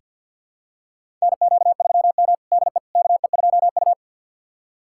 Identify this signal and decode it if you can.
Morse code, heard as a keyed tone, about 47 wpm, 700 Hz. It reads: NQ3M DE KE2U
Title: USA Callsign DE Callsign - 4 Characters - 50wpm